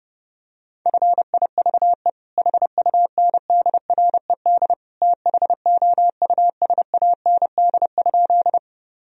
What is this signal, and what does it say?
Morse code, heard as a keyed tone, about 30 wpm, 700 Hz. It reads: FIVE HUNDRED THOUSAND?
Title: Sentences from Top 1000 Words - Review Part 3 - 30wpm